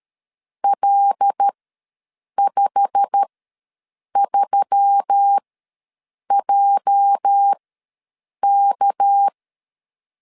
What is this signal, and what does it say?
Morse code, heard as a keyed tone, about 12 wpm, spaced as for 7 wpm, 800 Hz. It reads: L53JK
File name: LEZIONI / 10 SIGNS / 10A SLASH ws 12wpm 3443